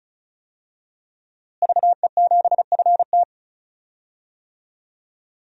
Morse code, heard as a keyed tone, about 36 wpm, 700 Hz. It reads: VE7FT
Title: DXCC-100 Call Signs - 5 Characters - Repeated 2x - 35wpm